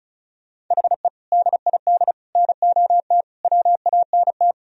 Morse code, heard as a keyed tone, about 44 wpm, 700 Hz. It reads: HE DID NOT WANT